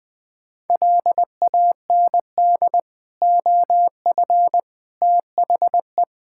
Morse code, heard as a keyed tone, about 20 wpm, 700 Hz. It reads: LAND OF THE
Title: Sets of 9 Words - 20wpm